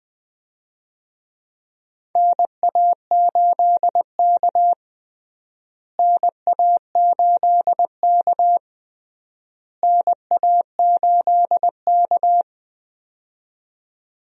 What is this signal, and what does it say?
Morse code, heard as a keyed tone, about 20 wpm, 700 Hz. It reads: NA8K NA8K NA8K
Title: USA Call Signs - 4 Characters - Repeated 3x - 20wpm